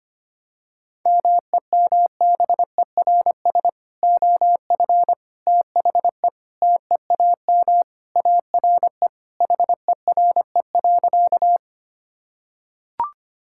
Morse code, heard as a keyed tone, about 25 wpm, 700 Hz. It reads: MEMBERS OF THE TEAM ARE HERE.
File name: practice / 25wpm / Sentences with 3-Word Phrases - 25wpm